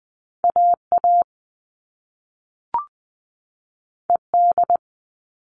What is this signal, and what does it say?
Morse code, heard as a keyed tone, about 20 wpm, 700 Hz. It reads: AA ED
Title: All Cut Numbers - 2-Digit 20wpm